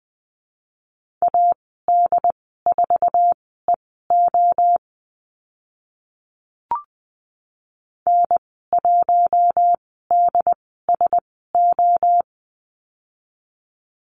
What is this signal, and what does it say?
Morse code, heard as a keyed tone, about 20 wpm, spaced as for 14 wpm, 700 Hz. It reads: AD4EO N1DSO
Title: 069 Lesson 12g - Letter d - Call Signs 20wpm